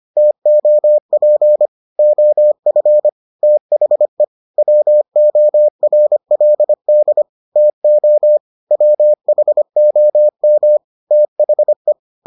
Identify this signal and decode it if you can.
Morse code, heard as a keyed tone, about 25 wpm, 600 Hz. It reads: TOP OF THE WORLD TO WHOM THE